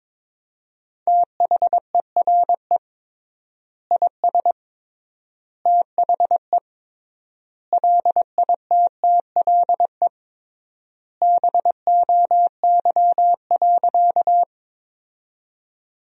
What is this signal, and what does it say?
Morse code, heard as a keyed tone, about 22 wpm, 700 Hz. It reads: THERE IS THE LITTLE BOY.
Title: 3x Word-Spacing - Sentences from Top 200 Words - 22wpm